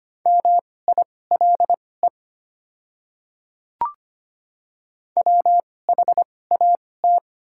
Morse code, heard as a keyed tone, about 25 wpm, spaced as for 18 wpm, 700 Hz. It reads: MILE WHAT
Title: 2x Character Spacing - Top 300 Words - 25wpm